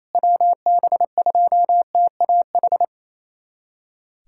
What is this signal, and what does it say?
Morse code, heard as a keyed tone, about 28 wpm, 700 Hz. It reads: WB2TAH
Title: USA Callsign DE Callsign - 6 Characters - 28wpm